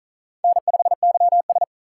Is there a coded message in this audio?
Yes: Morse code from 0.4 to 1.6 s, heard as a keyed tone, about 45 wpm, 700 Hz. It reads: N5YS